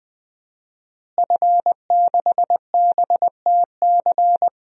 Morse code, heard as a keyed tone, about 20 wpm, 700 Hz. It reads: F6BTC
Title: DXCC-100 Call Signs - 5 Characters - Repeated 2x - 20wpm